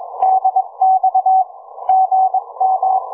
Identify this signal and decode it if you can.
Morse code, heard as a keyed tone, about 22 wpm, 750 Hz. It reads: DX GM
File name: cw forum / ve1zj 1830 160m 04h00